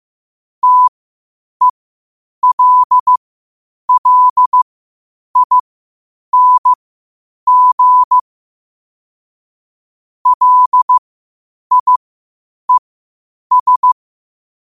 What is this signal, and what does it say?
Morse code, heard as a keyed tone, about 15 wpm, spaced as for 8 wpm, 1000 Hz. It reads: TELLING LIES
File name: qotdfiles / 8wpm